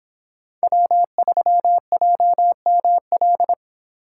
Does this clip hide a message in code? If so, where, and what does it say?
Yes: Morse code from 0.6 to 3.5 s, heard as a keyed tone, about 26 wpm, 700 Hz. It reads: W3JML